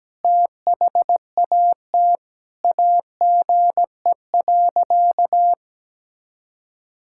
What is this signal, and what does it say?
Morse code, heard as a keyed tone, about 17 wpm, 700 Hz. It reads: THAT AGE.
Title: Sentences from Top 1000 Words - Review Part 6 - 17wpm